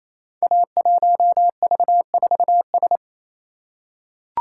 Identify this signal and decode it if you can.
Morse code, heard as a keyed tone, about 27 wpm, 700 Hz. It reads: A1V4S
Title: Code Groups - 5 Letter:Numbers - 28wpm